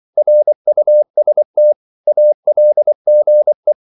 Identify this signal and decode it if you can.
Morse code, heard as a keyed tone, about 24 wpm, 600 Hz. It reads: RUST ALGE
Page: Korte danske ord 24wpm